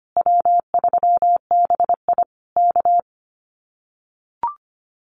Morse code, heard as a keyed tone, about 25 wpm, 700 Hz. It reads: W3BI K
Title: Common QSO Phrases Encore 25wpm